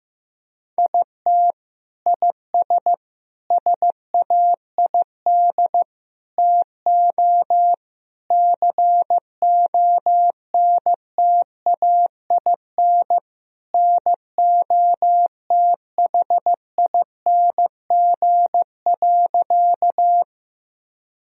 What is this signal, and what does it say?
Morse code, heard as a keyed tone, about 16 wpm, 700 Hz. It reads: IT IS SAID TO CONTAIN NOTHING.
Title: Sentences from Top 1000 Words - Review Part 2 - 15wpm